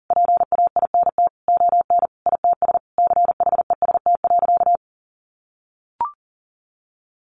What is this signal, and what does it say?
Morse code, heard as a keyed tone, about 40 wpm, 700 Hz. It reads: PAINT ON ITS CHEST.
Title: A Sound Of Thunder 40wpm